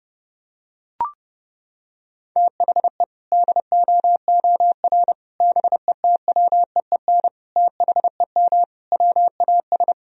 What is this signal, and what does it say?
Morse code, heard as a keyed tone, about 30 wpm, 700 Hz. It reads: THE DOOR BETWEEN THEM WAS
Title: Sentences from Top 1000 Words - Review Part 1 - 30wpm